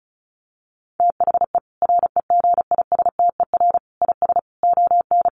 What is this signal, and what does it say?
Morse code, heard as a keyed tone, about 34 wpm, 700 Hz. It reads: THE REGISTER IS ON
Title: Sentences from Top 1200 Words 35wpm